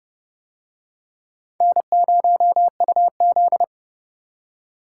Morse code, heard as a keyed tone, about 28 wpm, 700 Hz. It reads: N0UZ